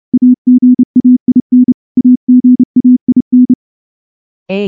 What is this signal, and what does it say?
Morse code, heard as a keyed tone, about 28 wpm, 250 Hz. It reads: AGAIN AGAIN
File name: morse-training-30WPM